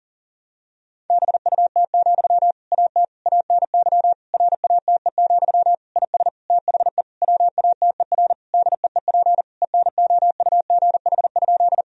Morse code, heard as a keyed tone, about 40 wpm, 700 Hz. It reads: BUT, AT ANY RATE, IS THE WATER DEEP ENOUGH?
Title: Sentences from Top 1000 Words - Review Part 4 - 40wpm